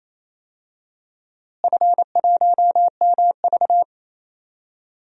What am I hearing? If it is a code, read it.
Morse code, heard as a keyed tone, about 27 wpm, 700 Hz. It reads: F1MV
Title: DXCC-100 Call Signs - 4 Characters - Repeated 2x - 28wpm